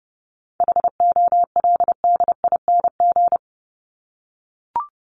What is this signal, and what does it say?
Morse code, heard as a keyed tone, about 30 wpm, 700 Hz. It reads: HOLDING
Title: ING Suffix - Top 100 Words with Base Words 30wpm